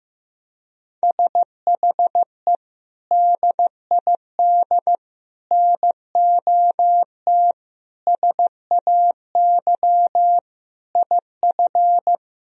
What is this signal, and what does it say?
Morse code, heard as a keyed tone, about 15 wpm, 700 Hz. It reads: SHE DID NOT SAY IF
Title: Sentences from Top 1000 Words - Review Part 3 - 15wpm